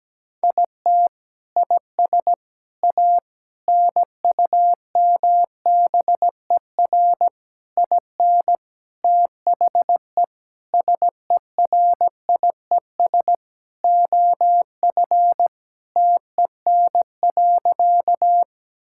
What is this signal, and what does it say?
Morse code, heard as a keyed tone, about 17 wpm, 700 Hz. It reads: IT IS A NUMBER IN THE SERIES OF TEN.
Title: Sentences from Top 1000 Words - Review Part 5 - 17wpm